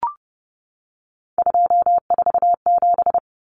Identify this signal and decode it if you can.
Morse code, heard as a keyed tone, about 30 wpm, 700 Hz. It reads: 247
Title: Numbers - 3-Digit 30wpm